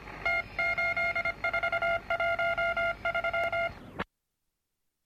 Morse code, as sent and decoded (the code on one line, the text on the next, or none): - ---.. ....- .---- ...--
T8413